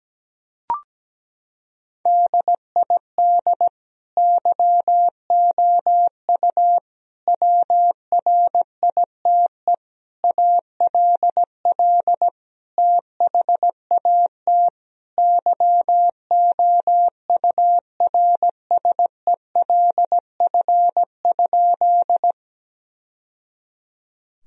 Morse code, heard as a keyed tone, about 17 wpm, 700 Hz. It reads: DID YOU WRITE ALL THAT YOURSELF?
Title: Sentences from Top 1000 Words - Review Part 1 - 17wpm